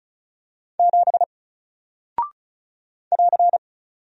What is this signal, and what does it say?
Morse code, heard as a keyed tone, about 36 wpm, 700 Hz. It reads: 7 <AR>
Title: Single Character (Letter:Number:Punctuation:Prosign) Alt 35wpm